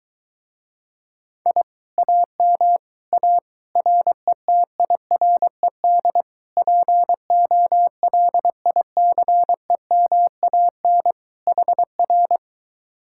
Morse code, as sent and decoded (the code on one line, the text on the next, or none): .. / .- -- / .- / .-. - .. .-. -.. / .--. --- .-.. .. -.-. -- .- -. / .... .-.
I AM A RETIRED POLICEMAN HR